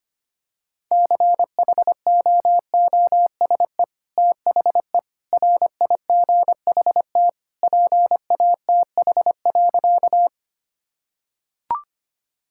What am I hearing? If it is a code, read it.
Morse code, heard as a keyed tone, about 25 wpm, 700 Hz. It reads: CHOOSE THE RIGHT PATH.
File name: practice / 25wpm / Sentences with 2-Word Phrases - 25wpm